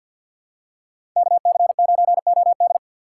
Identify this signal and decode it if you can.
Morse code, heard as a keyed tone, about 48 wpm, 700 Hz. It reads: KC9OD
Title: USA Call Signs - 5 Characters - 3x Character Spacing - 50wpm